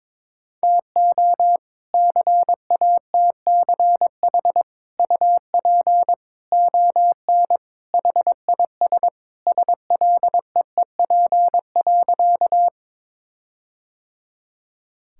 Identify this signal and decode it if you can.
Morse code, heard as a keyed tone, about 22 wpm, 700 Hz. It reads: TO CATCH UP ON HIS SLEEP.
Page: morse-code-ninja-practice-22wpm